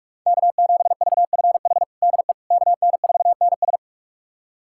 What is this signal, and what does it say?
Morse code, heard as a keyed tone, about 42 wpm, 700 Hz. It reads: K7VFH DE KN4NS
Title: USA Callsign DE Callsign - 5 Characters - 45wpm